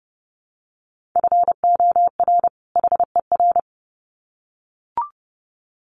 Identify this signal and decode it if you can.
Morse code, heard as a keyed tone, about 30 wpm, 700 Hz. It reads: FOR HER
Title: Sets of 2 Words 30wpm